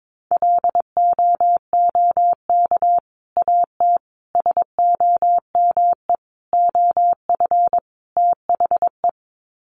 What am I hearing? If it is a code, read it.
Morse code, heard as a keyed tone, about 22 wpm, 700 Hz. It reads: LOOK AT SOME OF THE
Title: Sets of 5 Words Encore 22wpm